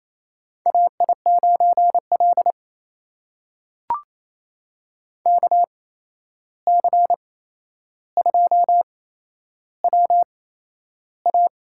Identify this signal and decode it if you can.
Morse code, heard as a keyed tone, about 28 wpm, spaced as for 11 wpm, 700 Hz. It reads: AI9L KC2WA
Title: USA Call Signs - 8x Character Spacing - 28wpm